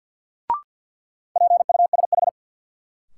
Morse code, heard as a keyed tone, about 52 wpm, 700 Hz. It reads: PUSH
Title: Sentences from Top 1300 Words - New Word Drill - 50wpm